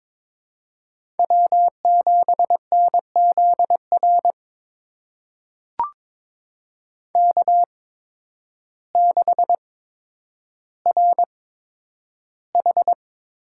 Morse code, heard as a keyed tone, about 22 wpm, spaced as for 10 wpm, 700 Hz. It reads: W7NZR K6RH
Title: USA Call Signs - 8x Character Spacing - 22wpm